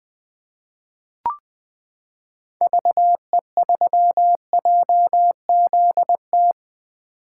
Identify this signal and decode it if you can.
Morse code, heard as a keyed tone, about 20 wpm, 700 Hz. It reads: VE3JZT
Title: 2023 POTA Activator Call Signs - 20wpm